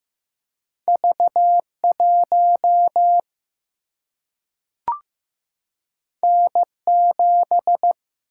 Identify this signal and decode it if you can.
Morse code, heard as a keyed tone, about 15 wpm, 700 Hz. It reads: V1 N7
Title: Single-Letter Single-Number 15wpm